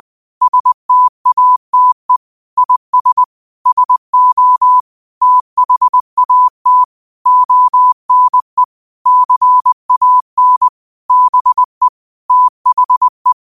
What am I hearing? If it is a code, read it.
Morse code, heard as a keyed tone, about 20 wpm, 1000 Hz. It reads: STATE IS SO THAT ONE CAN BE THE
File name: qotdfiles / 20wpm